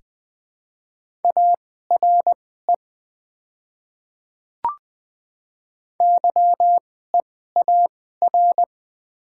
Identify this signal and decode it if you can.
Morse code, heard as a keyed tone, about 20 wpm, spaced as for 14 wpm, 700 Hz. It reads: ARE YEAR
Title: 2x Character Spacing - Top 200 Words - 20wpm